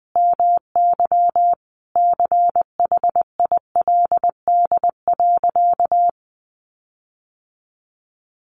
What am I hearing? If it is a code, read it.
Morse code, heard as a keyed tone, about 20 wpm, 700 Hz. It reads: MY CHILD.
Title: Sentences from Top 1300 Words 20wpm